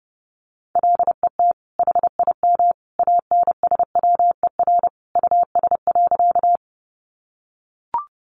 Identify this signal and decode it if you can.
Morse code, heard as a keyed tone, about 30 wpm, 700 Hz. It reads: LET HIM ANSWER US.
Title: Sentences from Top 300 Words 30wpm